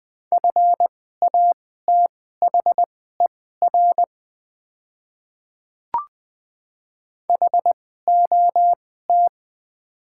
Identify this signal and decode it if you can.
Morse code, heard as a keyed tone, about 20 wpm, spaced as for 14 wpm, 700 Hz. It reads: FATHER HOT
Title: 2x Character Spacing - Top 200 Words - 20wpm